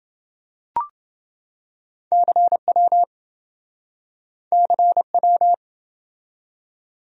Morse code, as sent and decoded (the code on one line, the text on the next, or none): -.-. .-- / -.-. .--
CW CW